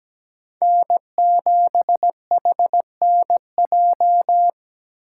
Morse code, heard as a keyed tone, about 20 wpm, 700 Hz. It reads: N7HNJ
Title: Speed Racing - USA 1x3 Call Signs 17wpm